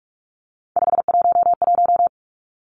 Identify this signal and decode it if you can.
Morse code, heard as a keyed tone, about 44 wpm, 700 Hz. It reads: 511
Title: Numbers - 3-Digit 45wpm